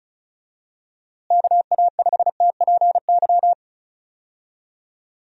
Morse code, heard as a keyed tone, about 36 wpm, 700 Hz. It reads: KA5TPY
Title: USA Call Signs - 6 Characters - Repeated 3x - 35wpm